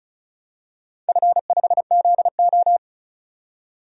Morse code, heard as a keyed tone, about 36 wpm, 700 Hz. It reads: F5ZO
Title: DXCC-100 Call Signs - 4 Characters - 35wpm